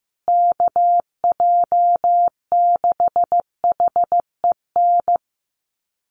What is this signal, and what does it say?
Morse code, heard as a keyed tone, about 15 wpm, 700 Hz. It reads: KJ6HEN
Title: USA Call Signs Repeated 2x 15wpm